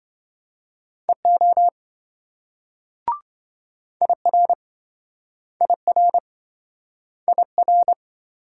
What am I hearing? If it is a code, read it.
Morse code, heard as a keyed tone, about 28 wpm, 700 Hz. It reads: EO IR IR IR